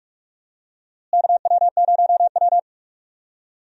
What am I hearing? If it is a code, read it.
Morse code, heard as a keyed tone, about 41 wpm, 700 Hz. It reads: KW0W
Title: USA Call Signs - 4 Characters - Repeated 3x - 45wpm